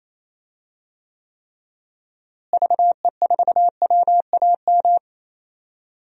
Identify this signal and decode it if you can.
Morse code, heard as a keyed tone, about 27 wpm, 700 Hz. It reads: VE4WAM
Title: DXCC-100 Call Signs - 6 Characters - Repeated 3x - 28wpm